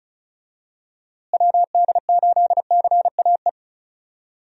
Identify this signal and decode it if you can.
Morse code, heard as a keyed tone, about 36 wpm, 700 Hz. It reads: WD8CAE
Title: USA Call Signs - 6 Characters - Repeated 3x - 35wpm